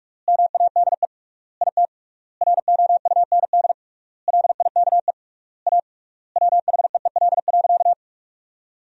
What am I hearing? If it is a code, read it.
Morse code, heard as a keyed tone, about 46 wpm, 700 Hz. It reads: MADE IT ROUND LIKE A WHEEL.
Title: 3x Word-Spacing - Sentences from Top 500 Words - 45wpm